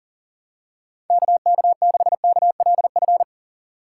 Morse code, heard as a keyed tone, about 40 wpm, 700 Hz. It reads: KK6KLF
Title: USA Call Signs - 6 Characters - 3x Character Spacing - 40wpm